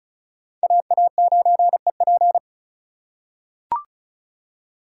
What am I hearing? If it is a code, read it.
Morse code, heard as a keyed tone, about 36 wpm, 700 Hz. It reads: AA9EP
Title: USA Call Signs - 5 Characters - 2x Character Spacing - 35wpm